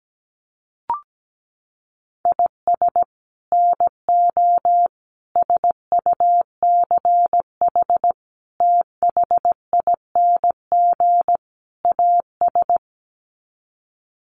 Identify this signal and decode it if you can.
Morse code, heard as a keyed tone, about 17 wpm, 700 Hz. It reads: IS NO SUCH THING AS
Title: Sets of 5 Words Encore 17wpm